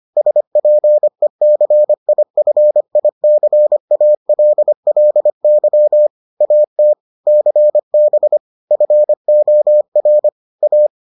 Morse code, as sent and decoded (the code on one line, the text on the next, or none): ... .--. -.-. .. ..-. .. -.-. .- .-.. .-.. -.-- / .- - / -.-. -... / ..-. --- .-. / .-
SPECIFICALLY AT CB FOR A